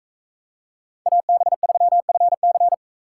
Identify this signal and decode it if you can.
Morse code, heard as a keyed tone, about 44 wpm, 700 Hz. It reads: AB3FC